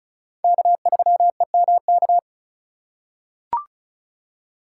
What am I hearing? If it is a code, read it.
Morse code, heard as a keyed tone, about 34 wpm, 700 Hz. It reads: K3EMK
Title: USA Call Signs - 5 Characters - 3x Character Spacing - 35wpm